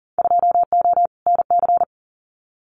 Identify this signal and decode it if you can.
Morse code, heard as a keyed tone, about 38 wpm, 700 Hz. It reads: 2O NC
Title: Winter Field Day Exchanges 40wpm